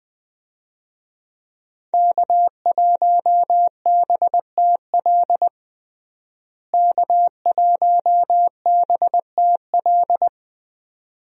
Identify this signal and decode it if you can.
Morse code, heard as a keyed tone, about 21 wpm, 700 Hz. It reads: K1BTL K1BTL
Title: USA Call Signs - 5 Characters - Repeated 2x - 20wpm